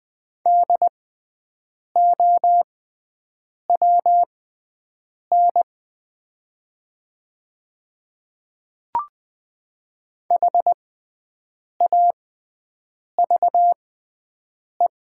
Morse code, heard as a keyed tone, about 20 wpm, spaced as for 7 wpm, 700 Hz. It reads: DOWN HAVE
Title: 6x Character Spacing - Top 100 Words - 20wpm